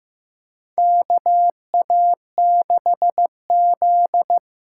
Morse code, heard as a keyed tone, about 15 wpm, 700 Hz. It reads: KA6Z
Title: USA Call Signs - 6 Characters - 6x Character Spacing - 15wpm